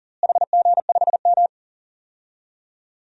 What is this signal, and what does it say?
Morse code, heard as a keyed tone, about 37 wpm, 700 Hz. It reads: HG5M
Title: DXCC-100 Call Signs - 4 Characters - Repeated 2x - 40wpm